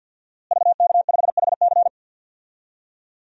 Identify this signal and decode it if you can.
Morse code, heard as a keyed tone, about 52 wpm, 700 Hz. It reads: VK5HC
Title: DXCC-100 Call Signs - 5 Characters - Repeated 3x - 50wpm